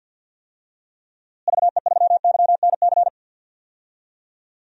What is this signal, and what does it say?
Morse code, heard as a keyed tone, about 52 wpm, 700 Hz. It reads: VE3YNC
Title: DXCC-100 Call Signs - 6 Characters - Repeated 2x - 50wpm